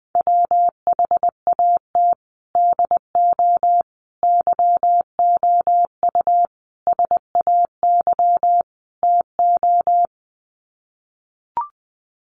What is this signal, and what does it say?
Morse code, heard as a keyed tone, about 19 wpm, 700 Hz. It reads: WHAT DO YOU SAY TO
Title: Sets of 5 Words 20wpm